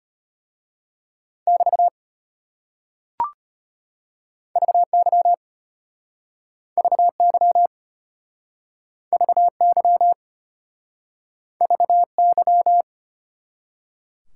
Morse code, heard as a keyed tone, about 32 wpm, 700 Hz. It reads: <BT> VY VY VY VY